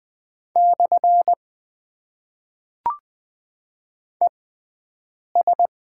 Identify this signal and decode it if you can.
Morse code, heard as a keyed tone, about 20 wpm, spaced as for 7 wpm, 700 Hz. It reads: / ES
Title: Top 90 QSO Elements - 6x Character Spacing - 20wpm